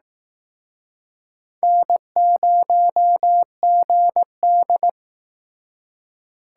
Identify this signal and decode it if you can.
Morse code, heard as a keyed tone, about 18 wpm, 700 Hz. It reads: N0GD
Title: Speed Racing - USA 1x2 Call Signs 15wpm